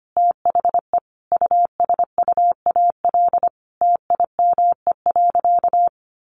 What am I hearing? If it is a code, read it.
Morse code, heard as a keyed tone, about 25 wpm, 700 Hz. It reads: THE USUAL TIME.
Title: Sentences from Top 400 Words 25wpm